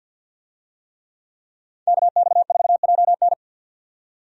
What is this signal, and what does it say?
Morse code, heard as a keyed tone, about 52 wpm, 700 Hz. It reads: KX4JN
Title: USA Call Signs - 5 Characters - 50wpm